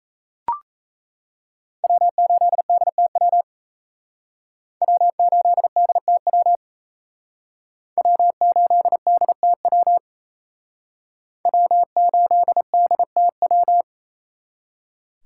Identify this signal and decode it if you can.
Morse code, heard as a keyed tone, about 35 wpm, 700 Hz. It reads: W8DTW W8DTW W8DTW W8DTW
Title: Speed Racing - USA 1x3 Call Signs 28wpm